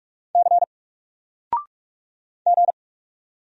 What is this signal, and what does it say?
Morse code, heard as a keyed tone, about 46 wpm, 700 Hz. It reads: C G
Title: Single Letter-Number 45wpm